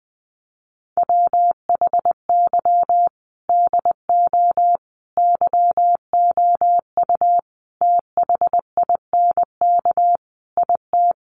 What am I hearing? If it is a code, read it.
Morse code, heard as a keyed tone, about 20 wpm, 700 Hz. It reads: WHY DO YOU THINK IT
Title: Sentences from Top 1400 Words 20wpm